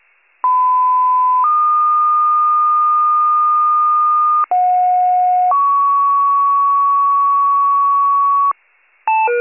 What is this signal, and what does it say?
Two-tone page, heard as a one-second tone, then a three-second tone, about 1010 then 1220 Hz, about 730 then 1120 Hz. It is heard from 0.4 to 8.5 s.